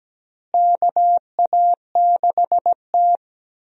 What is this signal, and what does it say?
Morse code, heard as a keyed tone, about 17 wpm, 700 Hz. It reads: KA6T
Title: USA Call Signs - 4 Characters - Repeated 2x - 17wpm